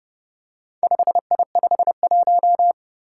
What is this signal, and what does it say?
Morse code, heard as a keyed tone, about 29 wpm, 700 Hz. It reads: 5I51
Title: Four Challenging Letter-Number Combinations 30wpm